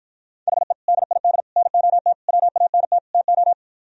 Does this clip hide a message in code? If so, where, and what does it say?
Yes: Morse code from 0.5 to 3.5 s, heard as a keyed tone, about 60 wpm, 700 Hz. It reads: HE DID NOT WANT TO